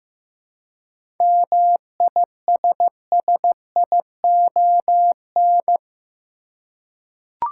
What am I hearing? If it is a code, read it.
Morse code, heard as a keyed tone, about 15 wpm, 700 Hz. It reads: MISSION
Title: Seven Letter Words - 15wpm